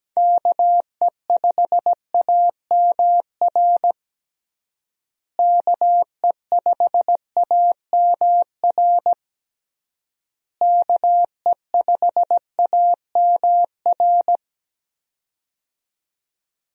Morse code, heard as a keyed tone, about 17 wpm, 700 Hz. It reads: KE5AMR KE5AMR KE5AMR
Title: USA Call Signs - 6 Characters - Repeated 3x - 17wpm